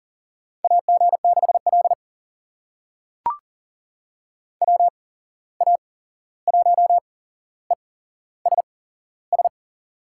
Morse code, heard as a keyed tone, about 40 wpm, spaced as for 15 wpm, 700 Hz. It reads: AG6L WA1ESS
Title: USA Call Signs - 8x Character Spacing - 40wpm